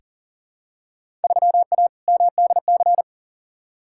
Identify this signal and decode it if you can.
Morse code, heard as a keyed tone, about 40 wpm, 700 Hz. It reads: 3A MDC
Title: Field Day Exchanges - 40wpm